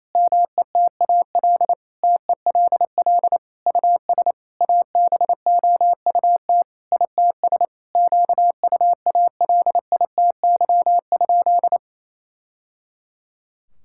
Morse code, heard as a keyed tone, about 28 wpm, 700 Hz. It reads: METAL TELL US ABOUT ITS QUALITY?
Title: Sentences from Top 1000 Words - Review Part 6 - 28wpm